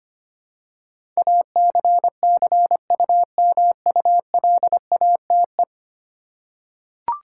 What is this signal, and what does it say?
Morse code, heard as a keyed tone, about 25 wpm, 700 Hz. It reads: ACCUMULATE
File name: 10 Letter Words - First 3 Letters Given - 25wpm